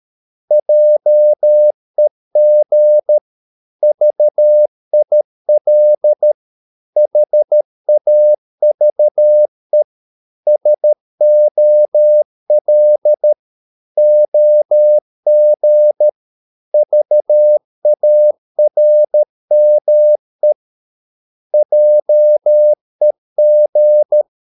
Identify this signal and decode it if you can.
Morse code, heard as a keyed tone, about 13 wpm, 600 Hz. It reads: JEG VIL HAVE SOL OG VARME JEG
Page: Ferie 13 wpm